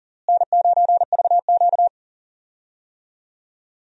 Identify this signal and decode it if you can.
Morse code, heard as a keyed tone, about 40 wpm, 700 Hz. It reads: N9VQ
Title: USA Call Signs - 4 Characters - Repeated 2x - 40wpm